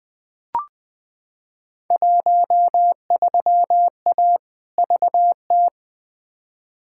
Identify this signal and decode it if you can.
Morse code, heard as a keyed tone, about 20 wpm, 700 Hz. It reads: 13A VT
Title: Field Day Exchanges Encore - 20wpm